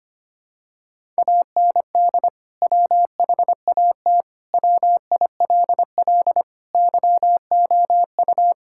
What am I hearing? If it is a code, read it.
Morse code, heard as a keyed tone, about 25 wpm, 700 Hz. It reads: AND WHAT WILL YOU
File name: practice / 25wpm / Sentences with 2-Word Phrases - 25wpm